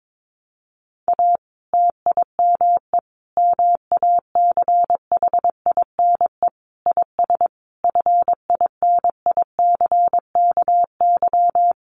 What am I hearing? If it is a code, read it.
Morse code, heard as a keyed tone, about 22 wpm, 700 Hz. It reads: A TIME MACHINE IS FINICKY
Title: A Sound Of Thunder 22wpm